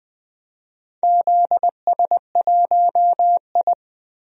Morse code, heard as a keyed tone, about 20 wpm, 700 Hz. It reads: ZS1I
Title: DXCC-100 Call Signs - 4 Characters - Repeated 3x - 20wpm